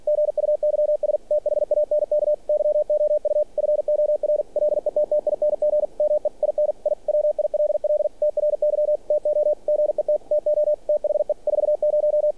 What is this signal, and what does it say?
Morse code, heard as a keyed tone, about 46 wpm, 600 Hz, with radio noise. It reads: GUYS THANK YOU FOR LETTING ME IN I WILL TRY TO GET TO THE 40
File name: archived / CRSnet